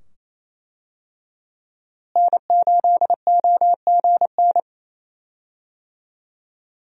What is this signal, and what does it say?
Morse code, heard as a keyed tone, about 27 wpm, 700 Hz. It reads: N8OGN
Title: USA Call Signs - 5 Characters - Repeated 2x - 28wpm